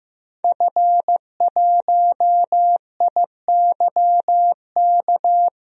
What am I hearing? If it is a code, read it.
Morse code, heard as a keyed tone, about 15 wpm, 700 Hz. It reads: F1IYK
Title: DXCC-100 Call Signs - 5 Characters - 15wpm